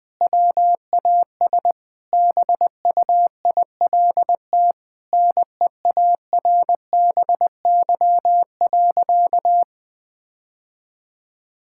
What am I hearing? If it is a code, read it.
Morse code, heard as a keyed tone, about 20 wpm, 700 Hz. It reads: WAS BUILT NEARBY.
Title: Sentences with 2-Word Phrases - 20wpm